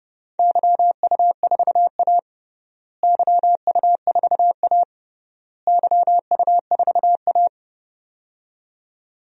Morse code, heard as a keyed tone, about 30 wpm, 700 Hz. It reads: YU4A YU4A YU4A
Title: DXCC-100 Call Signs - 4 Characters - Repeated 3x - 30wpm